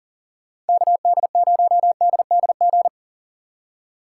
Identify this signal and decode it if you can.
Morse code, heard as a keyed tone, about 48 wpm, 700 Hz. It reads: KD0DDG
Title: Speed Racing - USA 2x3 Call Signs 40wpm